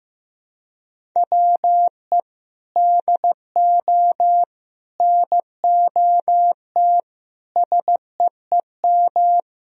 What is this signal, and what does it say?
Morse code, heard as a keyed tone, about 15 wpm, 700 Hz. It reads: WE DO NOT SEEM
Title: Sentences from Top 1000 Words - Review Part 1 - 15wpm